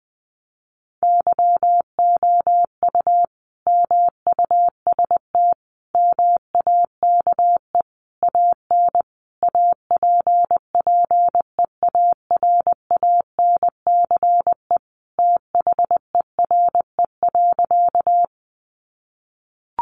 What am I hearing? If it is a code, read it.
Morse code, heard as a keyed tone, about 19 wpm, 700 Hz. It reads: YOU MUST MAKE AN APPEARANCE THERE.
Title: Sentences from Top 1100 Words 20wpm